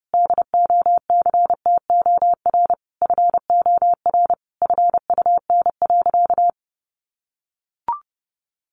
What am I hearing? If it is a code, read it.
Morse code, heard as a keyed tone, about 30 wpm, 700 Hz. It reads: DOCTOR FOR FUN.
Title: Additional Easy Phrases 30wpm